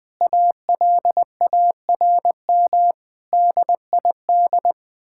Morse code, heard as a keyed tone, about 21 wpm, 700 Hz. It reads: ALARM DID
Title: Sentences with 4-Word Phrases - 20wpm